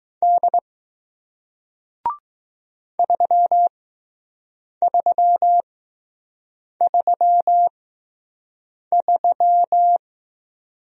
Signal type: Morse code

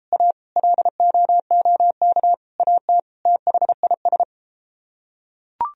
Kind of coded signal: Morse code

